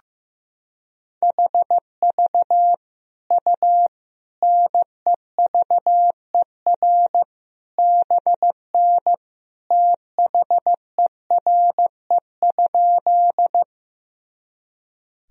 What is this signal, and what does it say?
Morse code, heard as a keyed tone, about 15 wpm, 700 Hz. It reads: HV U NEVER BN THERE?
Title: Sentences from Top 300 Words and Abbreviations 15wpm